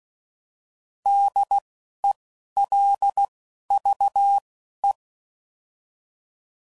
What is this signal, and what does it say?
Morse code, heard as a keyed tone, about 16 wpm, spaced as for 11 wpm, 800 Hz. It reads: DELVE
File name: Prefixes / By First Letter / Ds